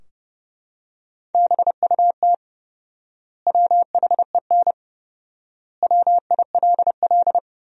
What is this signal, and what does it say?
Morse code, heard as a keyed tone, about 30 wpm, 700 Hz. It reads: BUT WHEN WILL
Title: 4x Word-Spacing - Sentences from Top 100 Words - 30wpm